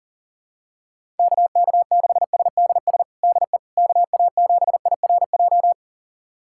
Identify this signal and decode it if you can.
Morse code, heard as a keyed tone, about 37 wpm, 700 Hz. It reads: KK6SDS DE KA7IRJ
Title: USA Callsign DE Callsign - 6 Characters - 40wpm